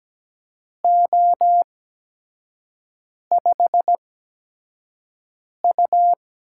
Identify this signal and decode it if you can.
Morse code, heard as a keyed tone, about 17 wpm, spaced as for 5 wpm, 700 Hz. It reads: O5U
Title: USA Call Signs - 4 Characters - 8x Character Spacing - 17wpm